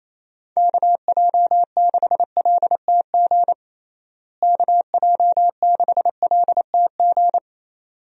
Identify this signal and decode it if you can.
Morse code, heard as a keyed tone, about 28 wpm, 700 Hz. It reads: KJ6LTG KJ6LTG
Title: USA Call Signs - 6 Characters - Repeated 2x - 28wpm